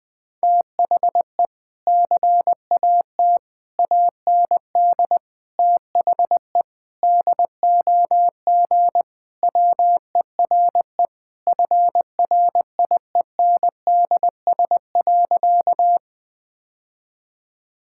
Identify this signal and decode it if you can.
Morse code, heard as a keyed tone, about 20 wpm, 700 Hz. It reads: THE CAT AND THE DOG WERE FRIENDS.
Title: Sentences with 2-Word Phrases - 20wpm